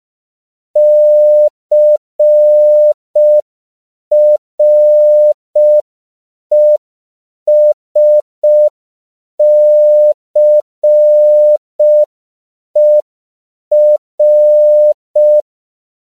Morse code, heard as a keyed tone, about 5 wpm, 600 Hz. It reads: CRESCER
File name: CW / 5PPM